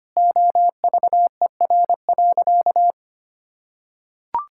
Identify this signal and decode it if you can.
Morse code, heard as a keyed tone, about 25 wpm, 700 Hz. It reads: OVER.
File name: Sentences with 3-Word Phrases - 25wpm